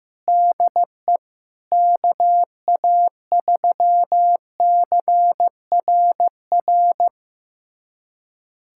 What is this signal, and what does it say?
Morse code, heard as a keyed tone, about 15 wpm, 700 Hz. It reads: DE KA3CRR
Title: USA Callsign DE Callsign - 6 Characters - 15wpm